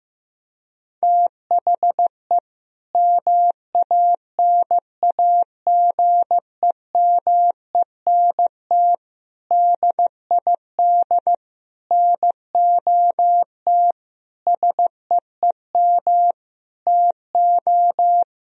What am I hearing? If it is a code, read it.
Morse code, heard as a keyed tone, about 15 wpm, 700 Hz. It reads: THE MANAGEMENT DID NOT SEEM TO
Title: Sentences from Top 1000 Words - Review Part 4 - 15wpm